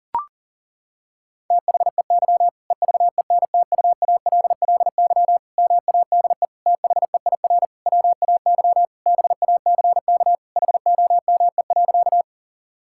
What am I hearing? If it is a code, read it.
Morse code, heard as a keyed tone, about 40 wpm, 700 Hz. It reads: THEY EVENTUALLY MADE THEIR WAY BACK HOME.
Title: Sentences from Top 1000 Words - Review Part 6 - 40wpm